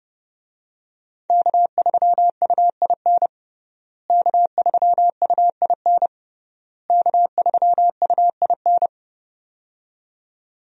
Morse code, heard as a keyed tone, about 30 wpm, 700 Hz. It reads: K3UIN K3UIN K3UIN
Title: USA Call Signs - 5 Characters - Repeated 3x - 30wpm